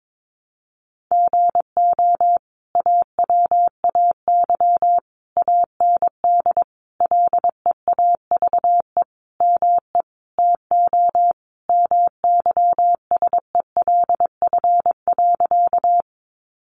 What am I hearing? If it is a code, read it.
Morse code, heard as a keyed tone, about 22 wpm, 700 Hz. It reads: GO AWAY AND LEAVE ME TO MYSELF.
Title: Sentences from Top 1300 Words 22wpm